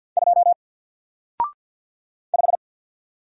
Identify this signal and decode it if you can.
Morse code, heard as a keyed tone, about 48 wpm, 700 Hz. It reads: 2 5
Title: Single Letter-Number 50wpm